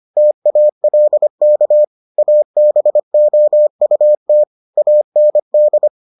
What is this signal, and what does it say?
Morse code, heard as a keyed tone, about 25 wpm, 600 Hz. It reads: TALK ABOUT AND